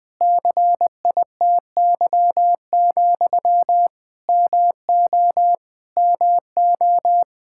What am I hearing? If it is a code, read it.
Morse code, heard as a keyed tone, about 21 wpm, 700 Hz. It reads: CITY, MO MO
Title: Practice QSO - Exchange 1 - Beginning - 20wpm